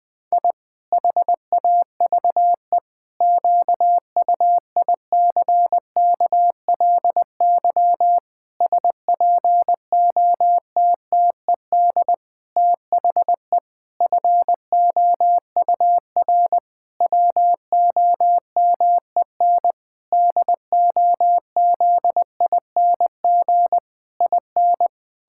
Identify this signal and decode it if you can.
Morse code, heard as a keyed tone, about 20 wpm, 700 Hz. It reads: I HAVE QUICKLY SPOTTED THE FOUR WOMEN DOZING IN
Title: Pangram Challenge v2 - 20wpm